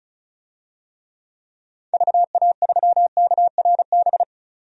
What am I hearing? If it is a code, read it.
Morse code, heard as a keyed tone, about 36 wpm, 700 Hz. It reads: VA3KRB
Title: DXCC-100 Call Signs - 6 Characters - 35wpm